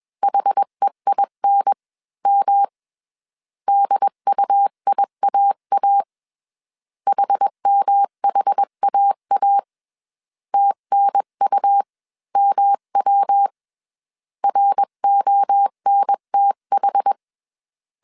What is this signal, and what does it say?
Morse code, heard as a keyed tone, about 21 wpm, spaced as for 19 wpm, 800 Hz. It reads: HEINM DUIAA HMHAA TNUMW RONTH
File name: CW / DET / 1 3 1500C S B 20wpm 1754